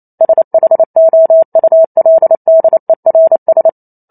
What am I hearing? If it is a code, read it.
Morse code, heard as a keyed tone, about 28 wpm, 650 Hz. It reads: SHOULDERS